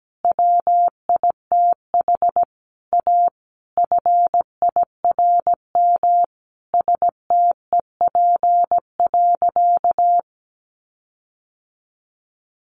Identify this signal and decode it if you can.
Morse code, heard as a keyed tone, about 17 wpm, 700 Hz. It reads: WITH A FIRM STEP.
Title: Sentences from Top 1400 Words 17wpm